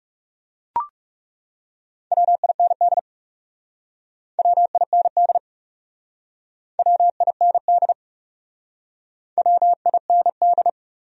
Morse code, heard as a keyed tone, about 38 wpm, 700 Hz. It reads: WIND WIND WIND WIND